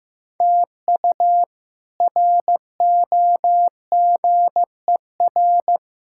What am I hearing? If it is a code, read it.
Morse code, heard as a keyed tone, about 16 wpm, 700 Hz. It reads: TU ROGER
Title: K1USN SST QSO Practice - 15wpm